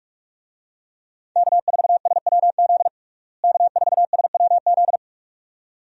Morse code, heard as a keyed tone, about 46 wpm, 700 Hz. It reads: K4SWZ K4SWZ
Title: USA Call Signs - 5 Characters - Repeated 2x - 45wpm